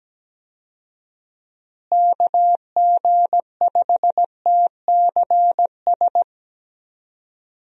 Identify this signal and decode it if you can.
Morse code, heard as a keyed tone, about 17 wpm, 700 Hz. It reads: KG5TCS
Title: USA Call Signs - 6 Characters - Repeated 2x - 17wpm